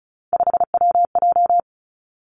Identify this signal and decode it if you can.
Morse code, heard as a keyed tone, about 36 wpm, 700 Hz. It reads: 5WJ
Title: Three Challenging Letter-Number Combinations 35wpm